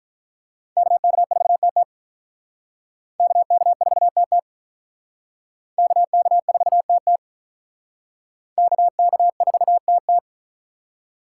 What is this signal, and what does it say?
Morse code, heard as a keyed tone, about 44 wpm, 700 Hz. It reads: KK4TT KK4TT KK4TT KK4TT